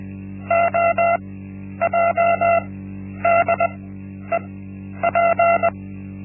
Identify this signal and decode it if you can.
Morse code, heard as a keyed tone, about 20 wpm, spaced as for 10 wpm, 680 Hz. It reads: OJDEP
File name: Linksmakalnis-1